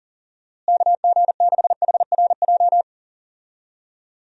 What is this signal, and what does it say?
Morse code, heard as a keyed tone, about 38 wpm, 700 Hz. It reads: KG6HRJ
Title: USA Call Signs - 6 Characters - Repeated 3x - 40wpm